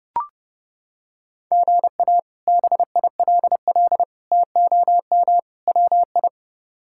Morse code, heard as a keyed tone, about 30 wpm, 700 Hz. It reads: GA BILL TOM WI
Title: K1USN SST QSO Practice - 30wpm